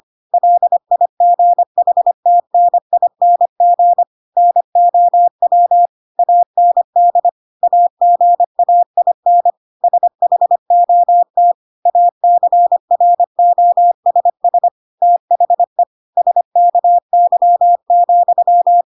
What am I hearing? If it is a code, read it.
Morse code, heard as a keyed tone, about 25 wpm, 700 Hz. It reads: LIGHTNING NOW AND AGAIN SHOT ACROSS THE SKY,
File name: Pinocchio / Pinocchio ch6 25 25 ik1hge